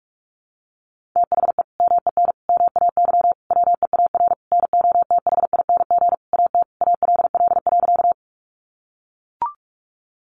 Morse code, heard as a keyed tone, about 44 wpm, 700 Hz. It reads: THE MEN MAY WEAR NOTHING AT ALL.
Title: Sentences from Top 1500 Words 45wpm